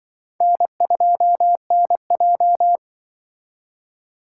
Morse code, heard as a keyed tone, about 24 wpm, 700 Hz. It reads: N2NJ
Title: Speed Racing - USA 1x2 Call Signs 20wpm